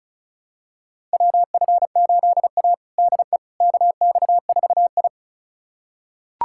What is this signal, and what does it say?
Morse code, heard as a keyed tone, about 36 wpm, 700 Hz. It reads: WF8A DE KX4I
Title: USA Callsign DE Callsign - 4 Characters - 35wpm